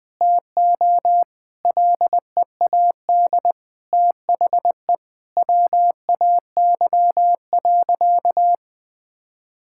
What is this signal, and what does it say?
Morse code, heard as a keyed tone, about 21 wpm, 700 Hz. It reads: TO LEAD THE WAY.
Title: Sentences from Top 1000 Words - Review Part 5 - 20wpm